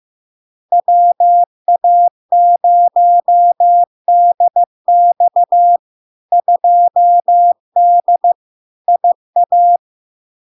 Morse code, heard as a keyed tone, about 15 wpm, 700 Hz. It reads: WA0DX 2D IA
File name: LARK-FD-2022-Log-in-CW-15wpm-700-Hz